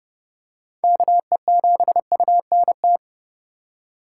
Morse code, heard as a keyed tone, about 30 wpm, 700 Hz. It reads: KE7UNT